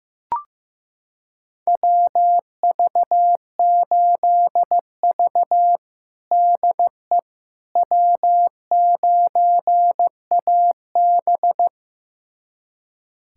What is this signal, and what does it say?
Morse code, heard as a keyed tone, about 15 wpm, 700 Hz. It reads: WV8V DE W9AB
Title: USA Callsign DE Callsign - 4 Characters - 15wpm